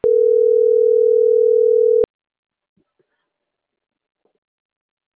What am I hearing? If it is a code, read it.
Ringback tone (North America), 440 + 480 Hz.